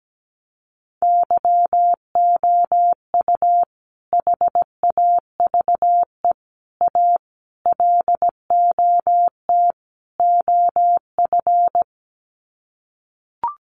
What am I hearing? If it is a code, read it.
Morse code, heard as a keyed tone, about 17 wpm, 700 Hz. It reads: YOU HAVE A LOT OF
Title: Sets of 5 Words Encore 17wpm